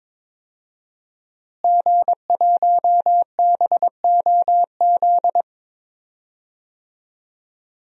Morse code, heard as a keyed tone, about 21 wpm, 700 Hz. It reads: G1BOZ
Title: DXCC-100 Call Signs - 5 Characters - Repeated 3x - 22wpm